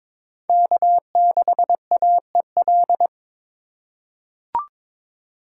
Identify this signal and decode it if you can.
Morse code, heard as a keyed tone, about 22 wpm, 690 Hz. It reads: K6AEL
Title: USA Call Signs - 5 Characters - 2x Character Spacing - 22wpm